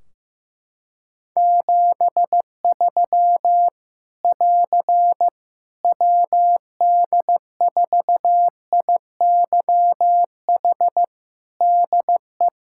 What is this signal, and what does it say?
Morse code, heard as a keyed tone, about 15 wpm, 700 Hz. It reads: 73 <AR> WD4IYH DE
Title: Practice QSO - Exchange 4 - Ending - 15wpm